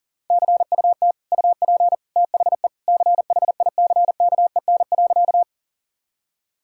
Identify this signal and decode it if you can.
Morse code, heard as a keyed tone, about 40 wpm, 700 Hz. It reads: CUT UP THE CHICKEN.
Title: Sentences from Top 1000 Words - Review Part 5 - 40wpm